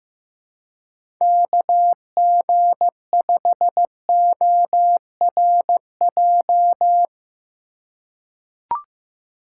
Morse code, heard as a keyed tone, about 15 wpm, 700 Hz. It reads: KG5ORJ
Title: USA Call Signs - 6 Characters - 4x Character Spacing - 15wpm